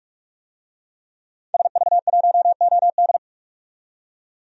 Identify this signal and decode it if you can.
Morse code, heard as a keyed tone, about 44 wpm, 700 Hz. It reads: SV1OD
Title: DXCC-100 Call Signs - 5 Characters - 45wpm